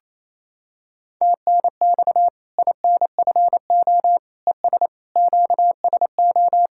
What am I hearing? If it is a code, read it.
Morse code, heard as a keyed tone, about 28 wpm, 700 Hz. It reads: TNX INFO ES QSO
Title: Practice QSO - Exchange 4 - Ending - 28wpm